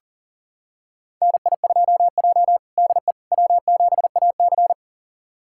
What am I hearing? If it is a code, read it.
Morse code, heard as a keyed tone, about 37 wpm, 700 Hz. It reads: NI2J DE W7AC
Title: USA Callsign DE Callsign - 4 Characters - 40wpm